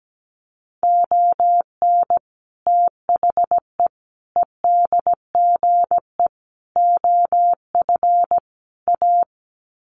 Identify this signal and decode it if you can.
Morse code, heard as a keyed tone, about 17 wpm, 700 Hz. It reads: ON THE EDGE OF A
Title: Sets of 5 Words Encore2 17wpm